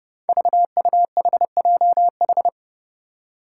Morse code, heard as a keyed tone, about 32 wpm, 700 Hz. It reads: VUHJH
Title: Five Challenging Letter-Number Combinations 30wpm